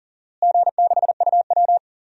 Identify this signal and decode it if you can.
Morse code, heard as a keyed tone, about 37 wpm, 700 Hz. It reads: G6UW
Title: DXCC-100 Call Signs - 4 Characters - Repeated 2x - 40wpm